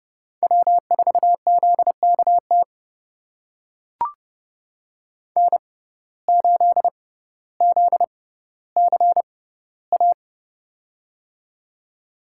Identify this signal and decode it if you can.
Morse code, heard as a keyed tone, about 30 wpm, spaced as for 15 wpm, 700 Hz. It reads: W4ZKT N8ZCA
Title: USA Call Signs - 5 Characters - 6x Character Spacing - 30wpm